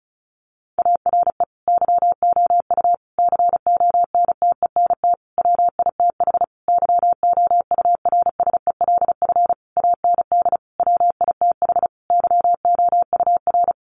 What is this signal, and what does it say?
Morse code, heard as a keyed tone, about 34 wpm, 700 Hz. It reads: ARE YOU CONTENT WITH YOURSELF AND WITH YOUR
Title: Sentences from Top 900 Words 35wpm